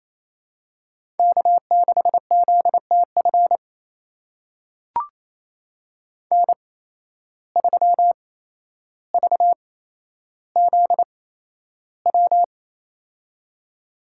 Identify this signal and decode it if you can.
Morse code, heard as a keyed tone, about 28 wpm, spaced as for 12 wpm, 700 Hz. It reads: K6ZTF N3VZW
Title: USA Call Signs - 5 Characters - 8x Character Spacing - 28wpm